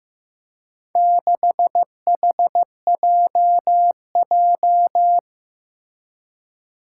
Morse code, heard as a keyed tone, about 15 wpm, 700 Hz. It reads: 6HJJ
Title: Four Challenging Letter-Number Combinations 15wpm